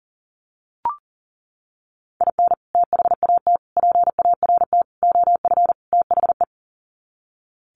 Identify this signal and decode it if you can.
Morse code, heard as a keyed tone, about 40 wpm, 700 Hz. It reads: IN THAT PART OF THE
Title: Sets of 5 Words Encore2 40wpm